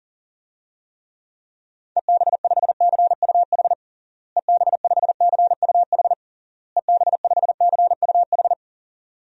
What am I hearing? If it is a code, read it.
Morse code, heard as a keyed tone, about 37 wpm, 700 Hz. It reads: EB5CUH EB5CUH EB5CUH
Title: DXCC-100 Call Signs - 6 Characters - Repeated 3x - 40wpm